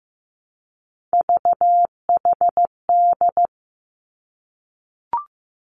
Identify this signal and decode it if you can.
Morse code, heard as a keyed tone, about 15 wpm, 700 Hz. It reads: VHD
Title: Three Challenging Letter-Number Combinations 15wpm